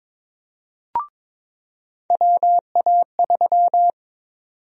Morse code, heard as a keyed tone, about 22 wpm, 700 Hz. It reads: WA3
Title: USA Call Sign Prefixes - 22wpm